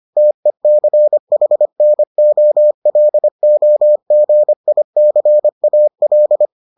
Morse code, heard as a keyed tone, about 25 wpm, 600 Hz. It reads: TECHNOLOGICAL